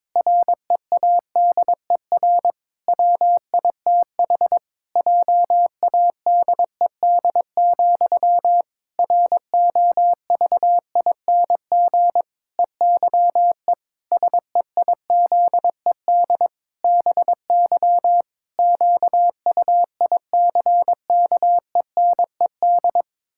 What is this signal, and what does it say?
Morse code, heard as a keyed tone, about 22 wpm, 700 Hz. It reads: READER WITH JADED, ROVING EYE SEIZED BY QUICKENED
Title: Pangram Challenge v2 - 22wpm